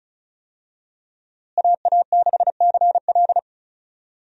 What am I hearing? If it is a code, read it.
Morse code, heard as a keyed tone, about 34 wpm, 700 Hz. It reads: AA6CL
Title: USA Call Signs - 5 Characters - Repeated 2x - 35wpm